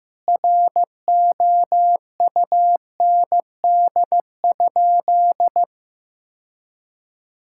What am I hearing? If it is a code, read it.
Morse code, heard as a keyed tone, about 15 wpm, 700 Hz. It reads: ROUND?
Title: Sentences from Top 1000 Words - Review Part 6 - 15wpm